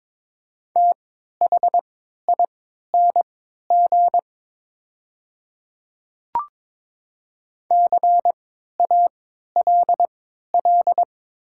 Morse code, heard as a keyed tone, about 22 wpm, spaced as for 12 wpm, 700 Hz. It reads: THING CALL
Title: 3x Character Spacing - Top 100 Words - 22wpm